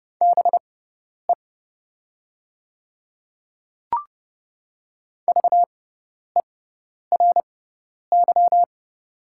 Morse code, heard as a keyed tone, about 30 wpm, spaced as for 10 wpm, 700 Hz. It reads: BE VERY
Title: 6x Character Spacing - Top 200 Words - 30wpm